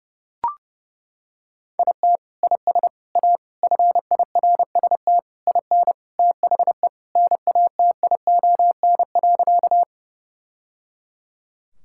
Morse code, heard as a keyed tone, about 30 wpm, 700 Hz. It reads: IT IS A FIRST IN THE NATION.
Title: Sentences from Top 1000 Words - Review Part 6 - 30wpm